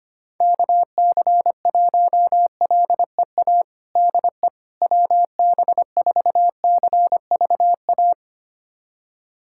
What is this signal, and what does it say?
Morse code, heard as a keyed tone, about 25 wpm, 700 Hz. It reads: KC1LEA DE WB4CVA
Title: USA Callsign DE Callsign 25wpm